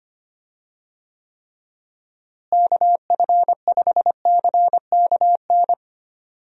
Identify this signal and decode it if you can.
Morse code, heard as a keyed tone, about 25 wpm, 700 Hz. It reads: KF5CKN
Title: USA Call Signs - 6 Characters - Repeated 2x - 25wpm